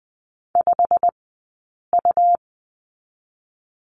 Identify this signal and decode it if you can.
Morse code, heard as a keyed tone, about 20 wpm, 700 Hz. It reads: 5 U
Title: 120 Lesson 20d - Number 3 - Two Characters 20wpm